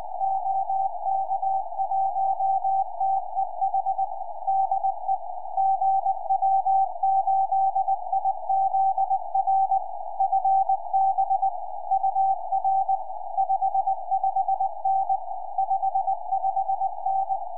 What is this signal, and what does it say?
Morse code, heard as a keyed tone, about 19 wpm, 770 Hz. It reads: OK1TEH DE GW8IZR FB UR 55N 55N
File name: gw8izr qso 4m